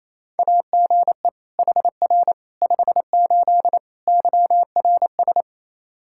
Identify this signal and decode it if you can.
Morse code, heard as a keyed tone, about 28 wpm, 700 Hz. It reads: AGE HR 58 YRS
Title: Practice QSO - Exchange 3 - More Info - 28wpm